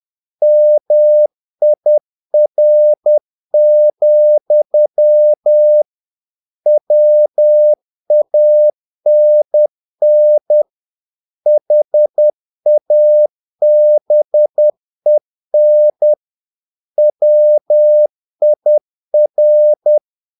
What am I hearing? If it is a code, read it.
Morse code, heard as a keyed tone, about 10 wpm, 600 Hz. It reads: MIR, WANN HABEN WIR
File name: dk5ke / Humor-50BpM